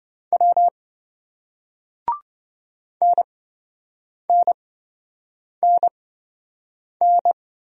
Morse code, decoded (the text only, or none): W NNNN